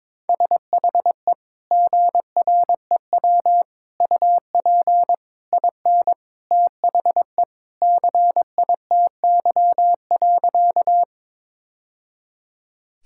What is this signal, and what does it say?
Morse code, heard as a keyed tone, about 22 wpm, 700 Hz. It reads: SHE GREW UP IN THE CITY.
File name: practice / 22wpm / Sentences with 2-Word Phrases - 22wpm